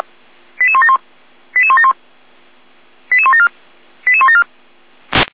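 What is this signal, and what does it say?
Five-tone selective call (ZVEI), heard as quick runs of five tones, tones of about 80 ms. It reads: 88171 88171 88174 88174
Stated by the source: ZVEI five-tone selective call received by SDR.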